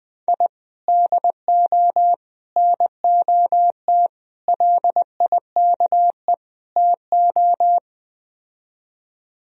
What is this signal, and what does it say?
Morse code, heard as a keyed tone, about 20 wpm, 700 Hz. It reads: I DO NOT LIKE TO